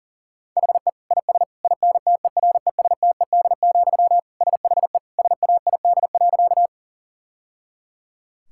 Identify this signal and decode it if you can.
Morse code, heard as a keyed tone, about 37 wpm, 700 Hz. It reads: HE IS INTERESTED, SHE SAID.
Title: Sentences from Top 1000 Words - Review Part 6 - 40wpm